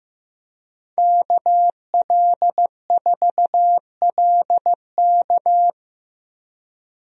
Morse code, heard as a keyed tone, about 15 wpm, 700 Hz. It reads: KL4LK
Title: USA Call Signs - 5 Characters - Repeated 3x - 15wpm